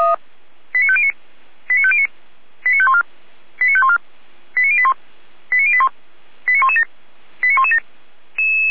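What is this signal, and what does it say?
Five-tone selective call (ZVEI), heard as quick runs of five tones, tones of about 70 ms. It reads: 89509 89509 89514 89514 89081 89081 89117 89117
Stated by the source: Received on SDR; ZVEI five-tone call.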